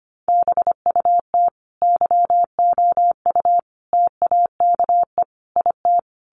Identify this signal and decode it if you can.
Morse code, heard as a keyed tone, about 25 wpm, 700 Hz. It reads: BUT YOU TAKE IT
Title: Sentences from Top 1200 Words 25wpm